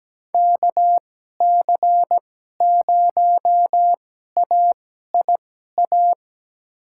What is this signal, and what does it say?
Morse code, heard as a keyed tone, about 17 wpm, spaced as for 12 wpm, 700 Hz. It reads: KC0AIA
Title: USA Call Signs - 6 Characters - 2x Character Spacing - 17wpm